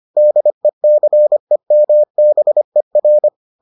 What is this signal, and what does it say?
Morse code, heard as a keyed tone, about 25 wpm, 600 Hz. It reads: DECEMBER